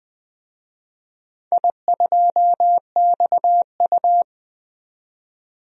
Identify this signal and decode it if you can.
Morse code, heard as a keyed tone, about 20 wpm, 700 Hz. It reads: I2XU
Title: DXCC-100 Call Signs - 4 Characters - 20wpm